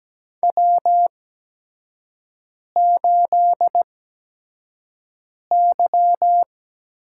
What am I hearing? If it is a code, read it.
Morse code, heard as a keyed tone, about 17 wpm, spaced as for 5 wpm, 700 Hz. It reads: W8Y
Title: USA Call Signs - 4 Characters - 8x Character Spacing - 17wpm